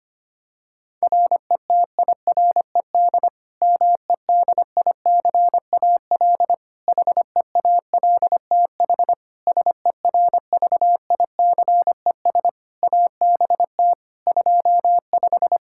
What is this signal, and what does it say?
Morse code, heard as a keyed tone, about 25 wpm, 700 Hz. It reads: RETIRED MEDICAL HEALTH SERVICES ABT 25
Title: Practice QSO - Exchange 3 - More Info - 25wpm